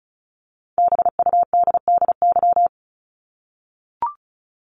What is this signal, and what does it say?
Morse code, heard as a keyed tone, about 34 wpm, 700 Hz. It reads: BUDDY
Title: Top 1751-2000 Words 35wpm